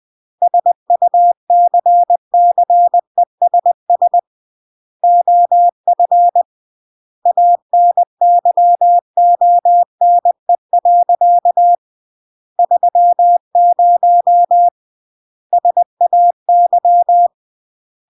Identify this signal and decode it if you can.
Morse code, heard as a keyed tone, about 20 wpm, 700 Hz. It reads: SUCCESS OF ANYONE. 30 SAY
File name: CW / jodorowsky / 82 commandments 20-20